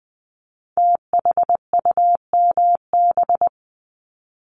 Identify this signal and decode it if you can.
Morse code, heard as a keyed tone, about 20 wpm, 700 Hz. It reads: THUMB
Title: Multi-letter Phonemes Repeated 3x 20wpm